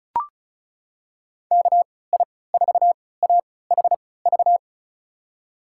Morse code, heard as a keyed tone, about 36 wpm, spaced as for 20 wpm, 700 Hz. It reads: KI4AHV
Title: USA Call Signs - 6 Characters - 3x Character Spacing - 35wpm